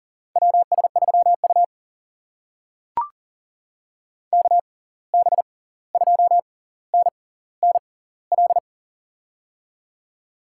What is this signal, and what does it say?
Morse code, heard as a keyed tone, about 38 wpm, spaced as for 18 wpm, 700 Hz. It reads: WS3U KB2NNL
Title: USA Call Signs - 6x Character Spacing - 40wpm